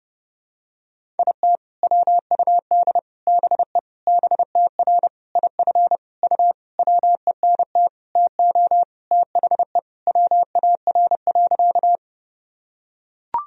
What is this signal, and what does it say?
Morse code, heard as a keyed tone, about 30 wpm, 700 Hz. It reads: IT WUD BE BTR IF U WENT TO THE WAR.
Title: Sentences from Top 400 Words and Abbreviations 30wpm